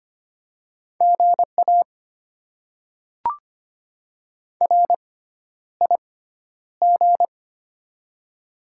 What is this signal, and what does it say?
Morse code, heard as a keyed tone, about 25 wpm, spaced as for 11 wpm, 700 Hz. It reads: GA RIG
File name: Top 20 QSO Elements - 6x Character Spacing - 25wpm